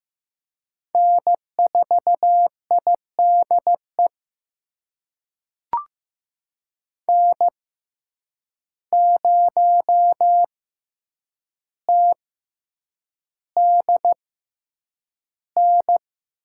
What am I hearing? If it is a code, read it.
Morse code, heard as a keyed tone, about 15 wpm, spaced as for 8 wpm, 700 Hz. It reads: N4IDE N0TDN
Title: USA Call Signs - 5 Characters - 6x Character Spacing - 15wpm